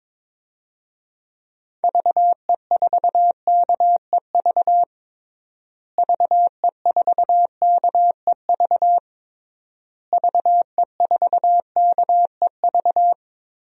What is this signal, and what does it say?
Morse code, heard as a keyed tone, about 22 wpm, 700 Hz. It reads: VE4KEV VE4KEV VE4KEV
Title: DXCC-100 Call Signs - 6 Characters - Repeated 3x - 22wpm